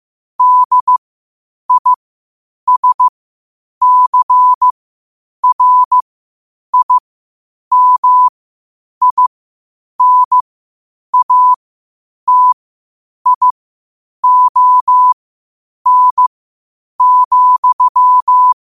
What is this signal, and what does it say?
Morse code, heard as a keyed tone, about 15 wpm, spaced as for 8 wpm, 1000 Hz. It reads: DISCRIMINATION,
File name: qotdfiles / 8wpm